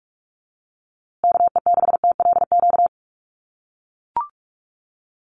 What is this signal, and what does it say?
Morse code, heard as a keyed tone, about 44 wpm, 700 Hz. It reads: KE6TLQ
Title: USA Call Signs Repeated 2x 45wpm